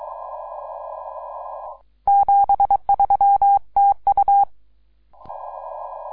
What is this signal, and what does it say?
Morse code, heard as a keyed tone, about 23 wpm, 800 Hz. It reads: 73 TU
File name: iw0ffk / 2012 04 02 3B8DB IW0FFK